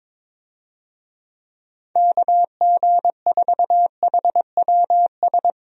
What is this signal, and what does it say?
Morse code, heard as a keyed tone, about 22 wpm, 700 Hz. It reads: KG4HWS
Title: USA Call Signs - 6 Characters - Repeated 2x - 22wpm